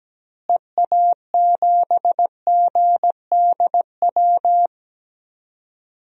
Morse code, heard as a keyed tone, about 17 wpm, 700 Hz. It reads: EA7GDW
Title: DXCC-100 Call Signs - 6 Characters - Repeated 2x - 17wpm